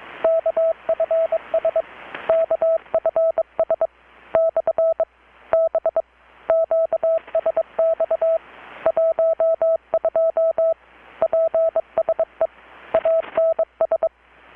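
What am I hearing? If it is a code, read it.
Morse code, heard as a keyed tone, about 22 wpm, 650 Hz. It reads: KFS KFS / B QSX 12 PSE ANS
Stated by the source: Morse from KPH's Night of Nights XX maritime radiotelegraphy event, received in Chiapas, Mexico about 3,800 km away on a Kenwood R-600 with a grounded 10 m long-wire antenna.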